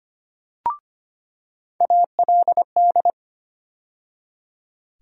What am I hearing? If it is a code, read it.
Morse code, heard as a keyed tone, about 25 wpm, 700 Hz. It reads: ALD
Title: USA Call Sign Postfixes - 3 Letters 25wpm